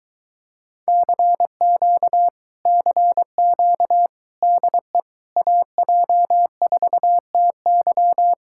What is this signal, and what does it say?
Morse code, heard as a keyed tone, about 23 wpm, 700 Hz. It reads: CQ CQ DE AJ4TY